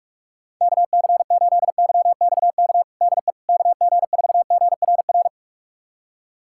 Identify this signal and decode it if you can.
Morse code, heard as a keyed tone, about 45 wpm, 700 Hz. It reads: KC8YXK DE KG4GRR
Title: USA Callsign DE Callsign - 6 Characters - 45wpm